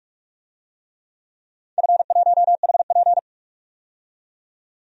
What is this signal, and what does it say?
Morse code, heard as a keyed tone, about 46 wpm, 700 Hz. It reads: F1HP
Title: DXCC-100 Call Signs - 4 Characters - 45wpm